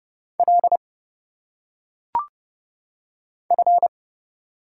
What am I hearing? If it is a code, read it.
Morse code, heard as a keyed tone, about 30 wpm, 700 Hz. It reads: L F
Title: Single Letters - Phonetic Pronunciation 30wpm